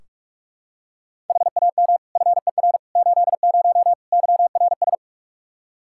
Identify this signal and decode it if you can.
Morse code, heard as a keyed tone, about 41 wpm, 700 Hz. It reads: HAM FER 80 YRS
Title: Practice QSO - Exchange 3 - More Info - 45wpm